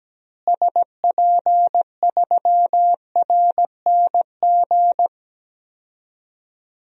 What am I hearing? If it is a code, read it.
Morse code, heard as a keyed tone, about 17 wpm, 700 Hz. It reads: SP3RNG
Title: DXCC-100 Call Signs - 6 Characters - Repeated 3x - 17wpm